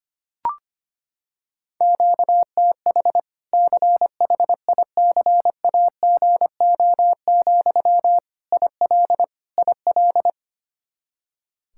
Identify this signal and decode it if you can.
Morse code, heard as a keyed tone, about 25 wpm, 700 Hz. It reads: QTH CHICAGO, IL IL
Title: Practice QSO - Exchange 1 - Beginning - 25wpm